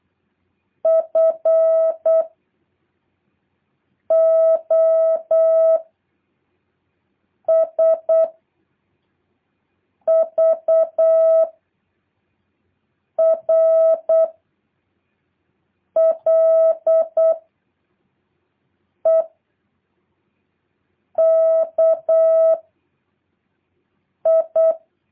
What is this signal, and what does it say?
Morse code, heard as a keyed tone, about 8 wpm, 650 Hz. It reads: F O S V R L E K I
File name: jh8cbh / 2023cw